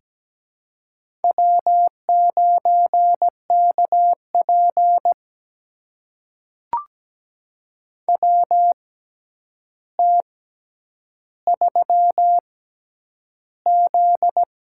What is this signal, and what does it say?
Morse code, heard as a keyed tone, about 17 wpm, spaced as for 9 wpm, 700 Hz. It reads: W9KP WT3Z
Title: USA Call Signs - 4 Characters - 6x Character Spacing - 17wpm